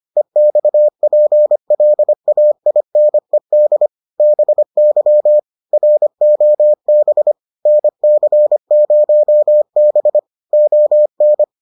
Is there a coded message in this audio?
Yes: Morse code from 0.2 to 11.4 s, heard as a keyed tone, about 25 wpm, 600 Hz. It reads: EXPLAINED BY ROB NC0B ON